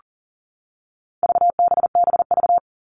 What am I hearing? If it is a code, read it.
Morse code, heard as a keyed tone, about 40 wpm, 700 Hz. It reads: VBBV
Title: All Cut Numbers - 4-Digit 40wpm